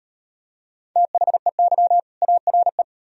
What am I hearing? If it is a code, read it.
Morse code, heard as a keyed tone, about 37 wpm, 700 Hz. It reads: THEY ARE